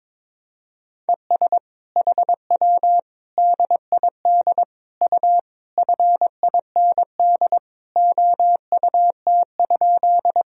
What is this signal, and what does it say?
Morse code, heard as a keyed tone, about 22 wpm, 700 Hz. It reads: ES HW DID U FIND OUT?
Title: Sentences from Top 100 Words and Abbreviations 22wpm